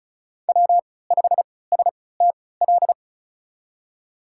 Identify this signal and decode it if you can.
Morse code, heard as a keyed tone, about 34 wpm, spaced as for 20 wpm, 700 Hz. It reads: W5STL
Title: USA Call Signs - 5 Characters - 3x Character Spacing - 35wpm